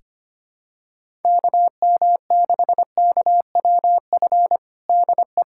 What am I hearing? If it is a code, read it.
Morse code, heard as a keyed tone, about 25 wpm, 700 Hz. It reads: KM6KWF DE
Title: USA Callsign DE Callsign - 6 Characters - 25wpm